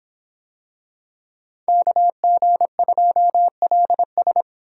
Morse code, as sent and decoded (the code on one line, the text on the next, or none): -.- --. ..--- .-.. ...
KG2LS